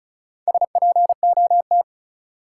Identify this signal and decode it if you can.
Morse code, heard as a keyed tone, about 34 wpm, 700 Hz. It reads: SPOT
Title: Sentences from Top 1100 Words - New Word Drill Alt - 35wpm